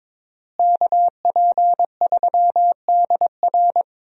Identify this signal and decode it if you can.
Morse code, heard as a keyed tone, about 21 wpm, 700 Hz. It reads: KP3DR
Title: USA Call Signs - 5 Characters - Repeated 2x - 22wpm